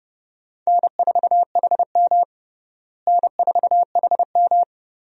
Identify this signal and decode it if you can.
Morse code, heard as a keyed tone, about 30 wpm, 700 Hz. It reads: N4HM N4HM
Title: USA Call Signs - 4 Characters - Repeated 3x - 30wpm